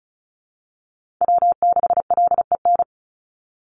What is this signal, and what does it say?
Morse code, heard as a keyed tone, about 36 wpm, 700 Hz. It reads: W6LEN
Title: Call Signs - 35wpm